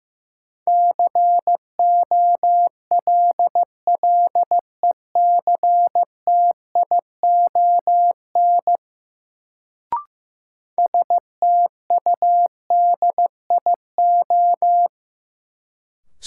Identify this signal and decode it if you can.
Morse code, heard as a keyed tone, about 15 wpm, 700 Hz. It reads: COLLECTION STUDIO
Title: Sentences from Top 1200 Words - New Word Drill - 15wpm